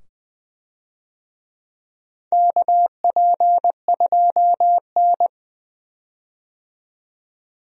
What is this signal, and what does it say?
Morse code, heard as a keyed tone, about 20 wpm, 700 Hz. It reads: KP2N
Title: DXCC-100 Call Signs - 4 Characters - 20wpm